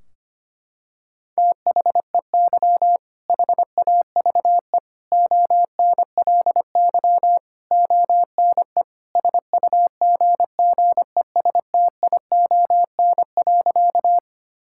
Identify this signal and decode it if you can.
Morse code, heard as a keyed tone, about 25 wpm, 700 Hz. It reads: THEY HAVE ONLY ONE SUGGESTION.
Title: Sentences from Top 1600 Words 25wpm